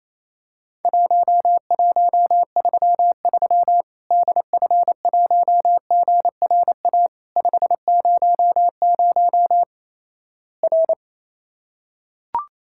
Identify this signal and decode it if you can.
Morse code, heard as a keyed tone, about 28 wpm, 700 Hz. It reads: 1133 DF1GRA 500 R
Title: Worked All Europe DX Contest - QTC Exchange - 28wpm